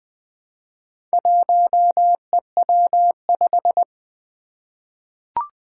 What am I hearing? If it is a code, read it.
Morse code, heard as a keyed tone, about 20 wpm, 700 Hz. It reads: 1EW5
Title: Four Challenging Letter-Number Combinations 20wpm